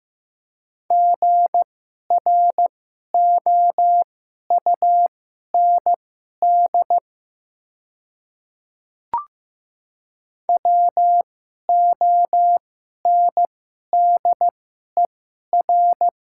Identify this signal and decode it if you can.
Morse code, heard as a keyed tone, about 15 wpm, spaced as for 11 wpm, 700 Hz. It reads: GROUND WONDER
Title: 2x Character Spacing - Top 500 Words - 15wpm